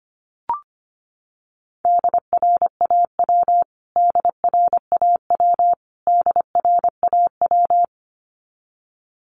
Morse code, heard as a keyed tone, about 25 wpm, 700 Hz. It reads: DRAW DRAW DRAW
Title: Top 300 Words Repeated 25wpm